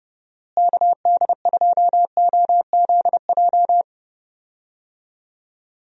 Morse code, heard as a keyed tone, about 31 wpm, 700 Hz. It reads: KD2OZJ
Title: USA Call Signs - 6 Characters - Repeated 2x - 30wpm